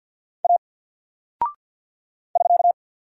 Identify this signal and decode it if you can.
Morse code, heard as a keyed tone, about 49 wpm, 700 Hz. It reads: A <SK>
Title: Single Character (Letter:Number:Punctuation:Prosign) Alt 50wpm